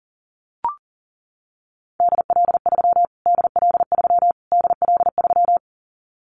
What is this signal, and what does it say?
Morse code, heard as a keyed tone, about 40 wpm, 700 Hz. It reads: DL3 DL3 DL3
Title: Callsign Prefixes for Top 25 DXCC - Repeated 3x - 40wpm